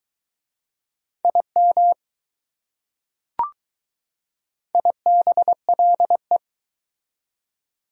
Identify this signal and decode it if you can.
Morse code, heard as a keyed tone, about 20 wpm, 700 Hz. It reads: IM IBLE